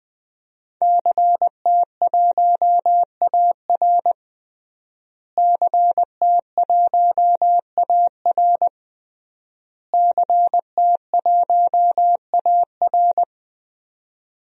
Morse code, heard as a keyed tone, about 20 wpm, 700 Hz. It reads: CT1AR CT1AR CT1AR
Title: DXCC-100 Call Signs - 5 Characters - Repeated 3x - 20wpm